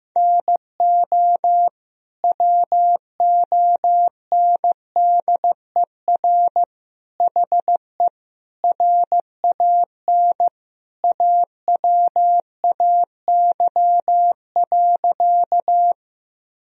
Morse code, heard as a keyed tone, about 15 wpm, 700 Hz. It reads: NO WONDER HE RAN AWAY.
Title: Sentences from Top 1000 Words - Review Part 3 - 15wpm